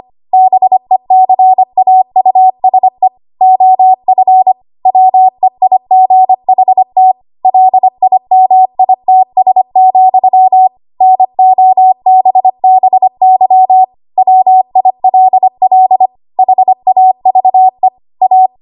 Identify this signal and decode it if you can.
Morse code, heard as a keyed tone, about 25 wpm, 750 Hz. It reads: BECAUSE OF WEIGHT LIMITS, NOBBY WILL HAVE A
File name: Archive / 25 WPM / 25WPM